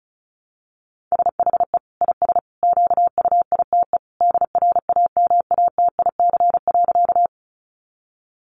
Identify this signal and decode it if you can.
Morse code, heard as a keyed tone, about 34 wpm, 700 Hz. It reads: SHE IS QUITE DRAMATIC.
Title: Sentences from Top 1300 Words 35wpm